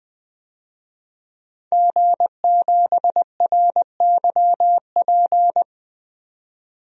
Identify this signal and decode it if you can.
Morse code, heard as a keyed tone, about 20 wpm, 700 Hz. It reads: G7RYP
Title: DXCC-100 Call Signs - 5 Characters - 20wpm